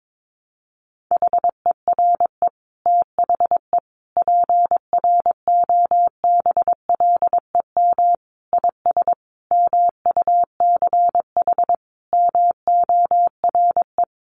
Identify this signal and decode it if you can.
Morse code, heard as a keyed tone, about 22 wpm, 700 Hz. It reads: HERE THE PROBLEM IS MUCH MORE
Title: Sentences from Top 1200 Words 22wpm